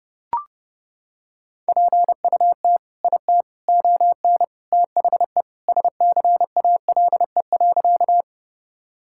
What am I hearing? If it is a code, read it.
Morse code, heard as a keyed tone, about 32 wpm, 700 Hz. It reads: PUT IT ON THE SCALE.
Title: Sentences from Top 1000 Words - Review Part 5 - 30wpm